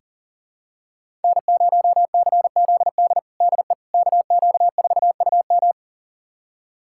Morse code, heard as a keyed tone, about 40 wpm, 700 Hz. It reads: N0CZD DE KQ4UM
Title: USA Callsign DE Callsign - 5 Characters - 40wpm